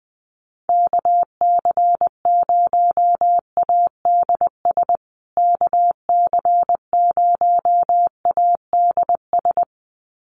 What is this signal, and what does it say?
Morse code, heard as a keyed tone, about 20 wpm, 700 Hz. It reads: KC0ADS KC0ADS
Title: USA Call Signs Repeated 2x 20wpm